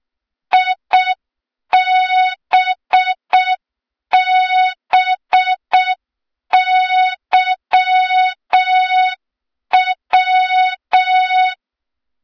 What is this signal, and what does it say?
Morse code, heard as a keyed tone, about 6 wpm, 3700 Hz, with noise. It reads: IBBYW
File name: Morze-HA2MN / hang / 23 GYAKORLAT